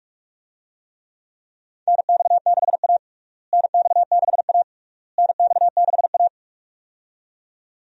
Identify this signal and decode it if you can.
Morse code, heard as a keyed tone, about 45 wpm, 700 Hz. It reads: NX6A NX6A NX6A
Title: USA Call Signs - 4 Characters - Repeated 3x - 45wpm